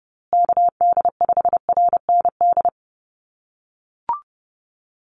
Morse code, heard as a keyed tone, about 30 wpm, 700 Hz. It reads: KD5RND
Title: USA Call Signs Repeated 2x 30wpm